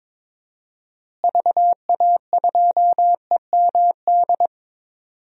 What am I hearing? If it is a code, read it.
Morse code, heard as a keyed tone, about 22 wpm, 700 Hz. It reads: VA2EMD
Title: DXCC-100 Call Signs - 6 Characters - Repeated 3x - 22wpm